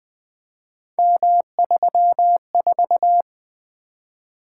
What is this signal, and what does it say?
Morse code, heard as a keyed tone, about 20 wpm, 700 Hz. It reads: M34
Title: Three Letter-Number Combinations - 20wpm